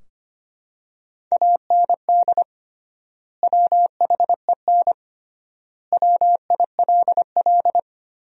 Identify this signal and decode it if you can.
Morse code, heard as a keyed tone, about 25 wpm, 700 Hz. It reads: AND WHEN WILL
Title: 3x Word-Spacing - Sentences from Top 100 Words - 25wpm